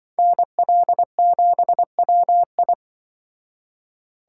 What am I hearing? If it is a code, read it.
Morse code, heard as a keyed tone, about 24 wpm, 700 Hz. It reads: NL7WI
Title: Speed Racing - USA 2x2 Call Signs 20wpm